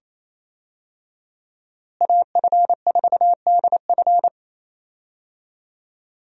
Morse code, heard as a keyed tone, about 28 wpm, 700 Hz. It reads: AF4DF
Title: USA Call Signs - 5 Characters - Repeated 2x - 28wpm